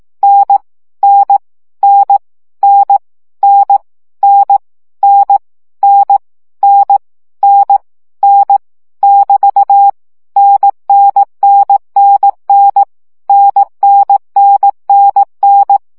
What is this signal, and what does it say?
Morse code, heard as a keyed tone, about 18 wpm, 800 Hz. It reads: N N N N N N N N N N N <BT> NNNNN NNNNN
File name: CW gyakorlas n 15 baud